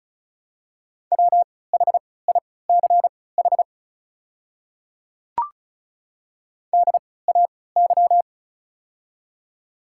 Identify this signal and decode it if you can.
Morse code, heard as a keyed tone, about 36 wpm, spaced as for 20 wpm, 690 Hz. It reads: WHICH DAY
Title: 3x Character Spacing - Top 100 Words - 35wpm